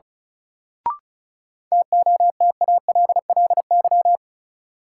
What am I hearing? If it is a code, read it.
Morse code, heard as a keyed tone, about 36 wpm, 700 Hz. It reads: TOTALLY
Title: Sentences from Top 1600 Words - New Word Drill Alt - 35wpm